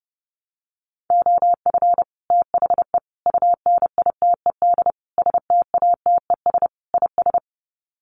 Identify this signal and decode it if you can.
Morse code, heard as a keyed tone, about 30 wpm, 700 Hz. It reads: OF THE UNITED STATES IS
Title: Sets of 5 Words Encore2 30wpm